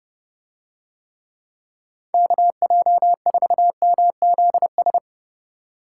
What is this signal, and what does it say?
Morse code, heard as a keyed tone, about 31 wpm, 700 Hz. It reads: KJ4MZS